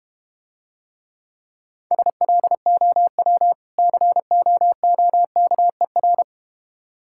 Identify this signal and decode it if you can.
Morse code, heard as a keyed tone, about 32 wpm, 700 Hz. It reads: SLOW COOKER